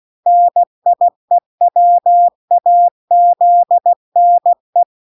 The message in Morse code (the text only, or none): NIEWAZNE